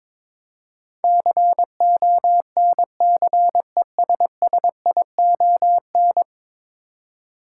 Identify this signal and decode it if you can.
Morse code, heard as a keyed tone, about 22 wpm, 700 Hz. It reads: CONCESSION
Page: morse-code-ninja-practice-22wpm